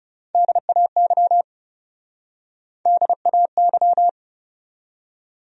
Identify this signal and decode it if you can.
Morse code, heard as a keyed tone, about 42 wpm, 700 Hz. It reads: DAY DAY